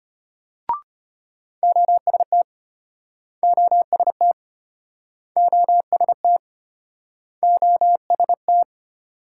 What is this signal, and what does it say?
Morse code, heard as a keyed tone, about 33 wpm, 700 Hz. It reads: OST OST OST OST